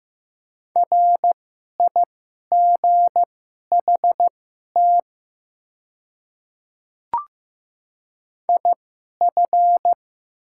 Morse code, heard as a keyed tone, about 15 wpm, spaced as for 11 wpm, 700 Hz. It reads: RIGHT IF
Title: 2x Character Spacing - Top 200 Words - 15wpm